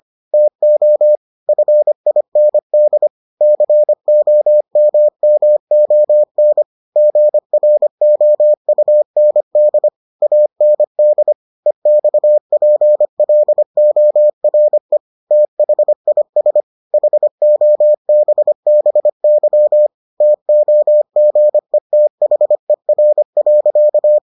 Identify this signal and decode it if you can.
Morse code, heard as a keyed tone, about 25 wpm, 600 Hz. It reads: TO FIND COMMON GROUND AND EXPLORE THIS HOBBY TOGETHER.